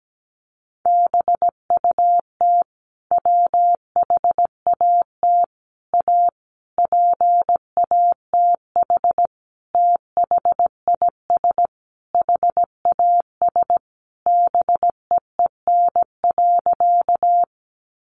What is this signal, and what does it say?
Morse code, heard as a keyed tone, about 17 wpm, 700 Hz. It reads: BUT WHAT A PATH THIS HAS BEEN.
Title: Sentences from Top 1400 Words 17wpm